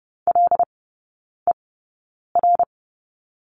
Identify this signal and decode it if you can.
Morse code, heard as a keyed tone, about 30 wpm, spaced as for 9 wpm, 700 Hz. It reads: LER
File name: Lesson 18e - Letter m - Three Characters 30wpm